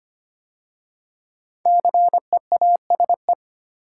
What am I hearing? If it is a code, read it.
Morse code, heard as a keyed tone, about 25 wpm, 700 Hz. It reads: CEASE
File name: 5 Letter Words - First 3 Letters Given - 25wpm